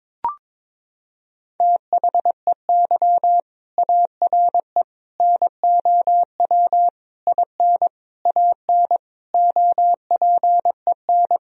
Morse code, heard as a keyed tone, about 22 wpm, 700 Hz. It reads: THEY ARE NOW IN AN OPEN
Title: Sentences from Top 1000 Words - Review Part 2 - 22wpm